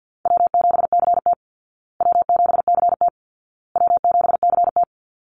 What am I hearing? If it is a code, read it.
Morse code, heard as a keyed tone, about 50 wpm, 700 Hz. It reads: W7CT W7CT W7CT
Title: Easy Call Signs Repeated 3x - 50wpm